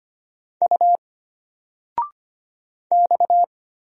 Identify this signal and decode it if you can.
Morse code, heard as a keyed tone, about 25 wpm, 700 Hz. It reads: U X
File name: Single-Letter-Number-25wpm